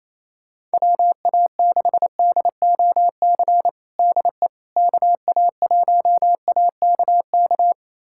Morse code, heard as a keyed tone, about 27 wpm, 700 Hz. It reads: WA6DOC DE KA1AKK
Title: USA Callsign DE Callsign - 6 Characters - 28wpm